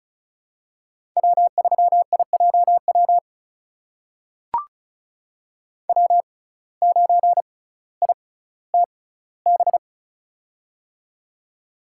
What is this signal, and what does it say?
Morse code, heard as a keyed tone, about 36 wpm, spaced as for 18 wpm, 700 Hz. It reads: W3IJW W9ITB
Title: USA Call Signs - 5 Characters - 6x Character Spacing - 35wpm